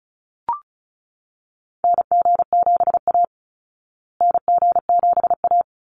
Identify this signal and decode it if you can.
Morse code, heard as a keyed tone, about 34 wpm, 700 Hz. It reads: NG7A NG7A
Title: Call Signs Repeated 3x - 35wpm